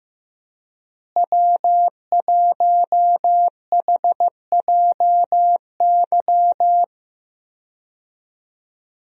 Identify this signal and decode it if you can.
Morse code, heard as a keyed tone, about 15 wpm, 700 Hz. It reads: W1HJY
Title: USA Call Signs - 5 Characters - Repeated 2x - 15wpm